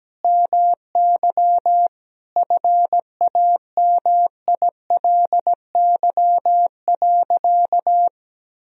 Morse code, heard as a keyed tone, about 17 wpm, 700 Hz. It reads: MY FAMILY.
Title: Sentences from Top 1000 Words - Review Part 2 - 17wpm